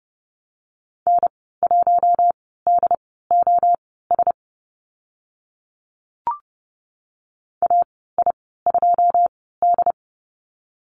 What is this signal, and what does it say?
Morse code, heard as a keyed tone, about 28 wpm, spaced as for 17 wpm, 700 Hz. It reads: N1DOS AI2D
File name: Lesson 14g - Number 2 - Call Signs 30wpm